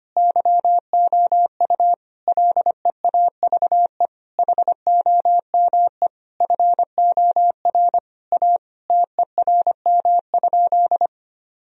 Morse code, heard as a keyed tone, about 25 wpm, 700 Hz. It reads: YOU LEAVE HOME FOR A TERM?
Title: Sentences from Top 1000 Words - Review Part 4 - 25wpm